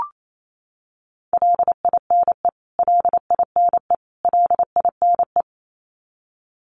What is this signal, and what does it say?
Morse code, heard as a keyed tone, about 28 wpm, 700 Hz. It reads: LINE LINE LINE
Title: Top 300 Words Repeated 28wpm